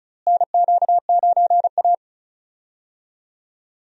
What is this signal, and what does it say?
Morse code, heard as a keyed tone, about 36 wpm, 700 Hz. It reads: NQ9A
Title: USA Call Signs - 4 Characters - Repeated 2x - 35wpm